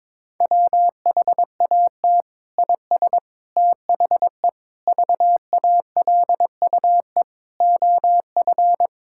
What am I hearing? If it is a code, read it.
Morse code, heard as a keyed tone, about 22 wpm, 700 Hz. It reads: WHAT IS THE VALUE OF
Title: Sentences from Top 1000 Words - Review Part 6 - 22wpm